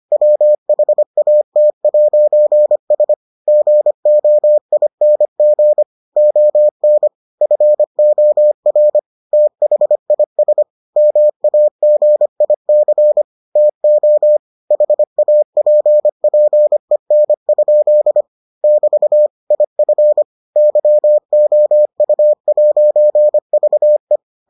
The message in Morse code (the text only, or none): WHAT'S GOING ON FOR THIS MAGIC TO HAPPEN? <BT> IF YOU'VE